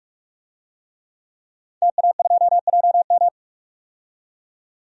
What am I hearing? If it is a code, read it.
Morse code, heard as a keyed tone, about 46 wpm, 700 Hz. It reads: TA2JM
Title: DXCC-100 Call Signs - 5 Characters - 45wpm